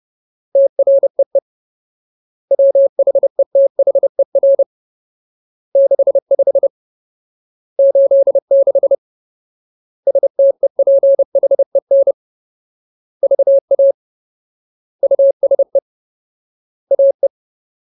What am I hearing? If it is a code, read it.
Morse code, heard as a keyed tone, about 30 wpm, 550 Hz. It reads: TREE WHETHER 65 86 STEPHEN VA USE AE
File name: Ultimate Word List / 30WPM Ultimate